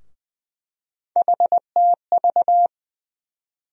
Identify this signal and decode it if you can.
Morse code, heard as a keyed tone, about 20 wpm, 700 Hz. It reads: HTV
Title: USA Call Sign Postfixes - 3 Letters 20wpm